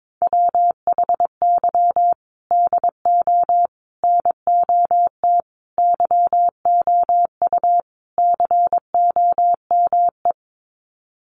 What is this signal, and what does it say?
Morse code, heard as a keyed tone, about 22 wpm, 700 Hz. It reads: WHY DO NOT YOU COME
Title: Sets of 5 Words Encore2 22wpm